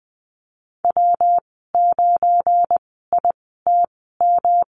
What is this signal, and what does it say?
Morse code, heard as a keyed tone, about 20 wpm, spaced as for 15 wpm, 700 Hz. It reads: W9ITM
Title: 173 Lesson 28g - Number 9 - Call Signs 20wpm